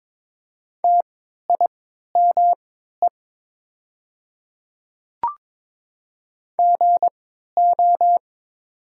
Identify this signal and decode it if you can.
Morse code, heard as a keyed tone, about 22 wpm, spaced as for 12 wpm, 700 Hz. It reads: TIME GO
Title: 3x Character Spacing - Top 300 Words - 22wpm